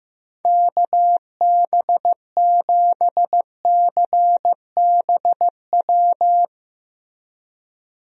Morse code, heard as a keyed tone, about 15 wpm, 700 Hz. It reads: KB7CBW
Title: USA Call Signs - 6 Characters - Repeated 3x - 15wpm